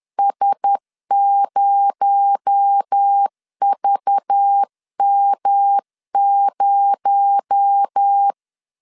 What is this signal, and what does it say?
Morse code, heard as a keyed tone, about 10 wpm, 800 Hz. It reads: S0VM0
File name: CW / DET / 1 6 1500C S B 10wpm 4137